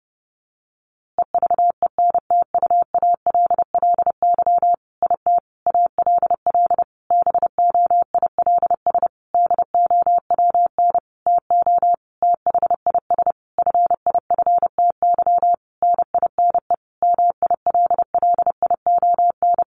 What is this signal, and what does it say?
Morse code, heard as a keyed tone, about 30 wpm, 700 Hz. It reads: EVENTUALLY IT ALL BOILS DOWN TO THIS FIFTY NINE MILLION
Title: A Sound Of Thunder 30wpm